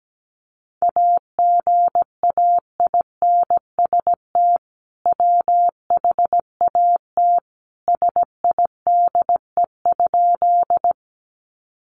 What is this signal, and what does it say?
Morse code, heard as a keyed tone, about 17 wpm, 700 Hz. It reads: AGAINST WHAT SIDE?
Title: Sentences from Top 400 Words 17wpm